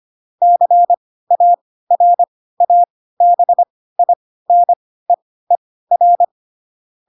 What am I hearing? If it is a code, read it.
Morse code, heard as a keyed tone, about 25 wpm, spaced as for 16 wpm, 700 Hz. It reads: CARABINEER
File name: Pinocchio / Pinocchio ch3 10 25 ik1hge